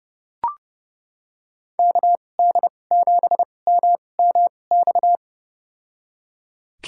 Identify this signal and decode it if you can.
Morse code, heard as a keyed tone, about 30 wpm, spaced as for 22 wpm, 700 Hz. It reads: KD7MMX
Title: USA Call Signs - 6 Characters - 2x Character Spacing - 30wpm